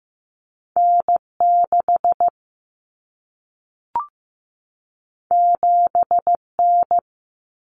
Morse code, heard as a keyed tone, about 15 wpm, 700 Hz. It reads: N6 7N
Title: Common Cut Numbers - 2-Digit 15wpm